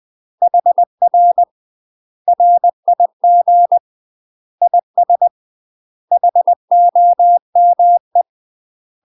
Morse code, heard as a keyed tone, about 20 wpm, 700 Hz. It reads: HR RIG IS HOME
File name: UK MASTER QSOs TRAINING / 20WPM / UK MASTER QSOs 0051